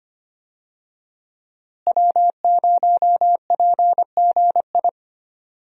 Morse code, heard as a keyed tone, about 25 wpm, 700 Hz. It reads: W0PGI
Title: USA Call Signs - 5 Characters - Repeated 3x - 25wpm